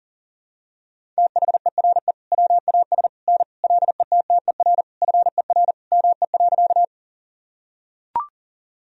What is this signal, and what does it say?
Morse code, heard as a keyed tone, about 40 wpm, 700 Hz. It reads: THERE WAS N LETTER FER ME.
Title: Sentences from Top 300 Words and Abbreviations 40wpm